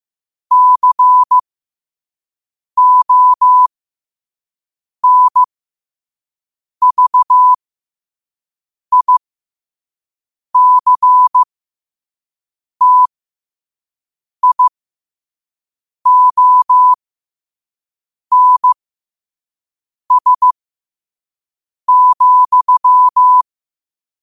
Morse code, heard as a keyed tone, about 15 wpm, spaced as for 5 wpm, 1000 Hz. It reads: CONVICTIONS,
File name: qotdfiles / 5wpm